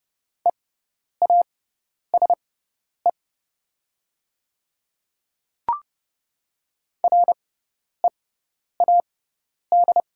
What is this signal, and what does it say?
Morse code, heard as a keyed tone, about 30 wpm, spaced as for 10 wpm, 690 Hz. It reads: EASE READ
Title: 6x Character Spacing - Top 300 Words - 30wpm